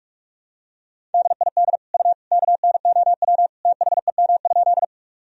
Morse code, heard as a keyed tone, about 46 wpm, 700 Hz. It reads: DID U KNOW THEM?
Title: Sentences from Top 100 Words and Abbreviations 45wpm